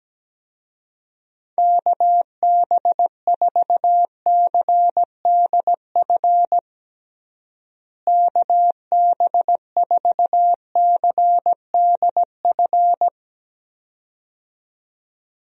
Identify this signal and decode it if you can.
Morse code, heard as a keyed tone, about 17 wpm, 700 Hz. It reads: KB4CDF KB4CDF
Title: USA Call Signs - 6 Characters - Repeated 2x - 17wpm